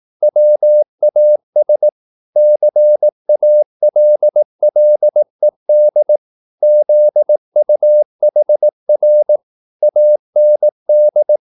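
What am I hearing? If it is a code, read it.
Morse code, heard as a keyed tone, about 18 wpm, 600 Hz. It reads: WAS CALLED ZUHR AND
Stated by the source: Morse practice at 18 wpm, English text, from CW med Gnister.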